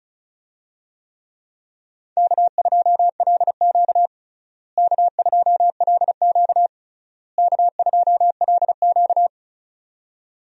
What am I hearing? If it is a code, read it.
Morse code, heard as a keyed tone, about 36 wpm, 700 Hz. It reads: K2LQ K2LQ K2LQ
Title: USA Call Signs - 4 Characters - Repeated 3x - 35wpm